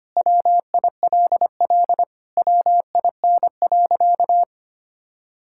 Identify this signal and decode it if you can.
Morse code, heard as a keyed tone, about 25 wpm, 700 Hz. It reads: WILL WIN.
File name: practice / 25wpm / Sentences with 3-Word Phrases - 25wpm